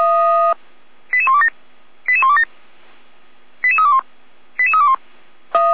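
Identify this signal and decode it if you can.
Five-tone selective call (ZVEI), heard as quick runs of five tones, tones of about 70 ms. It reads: 88127 88127 88321 88321
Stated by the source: ZVEI five-tone selective call 88289 received on SDR.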